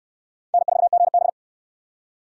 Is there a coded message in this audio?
Yes: Morse code from 0.5 to 1.3 s, heard as a keyed tone, about 68 wpm, 700 Hz. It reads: N4GB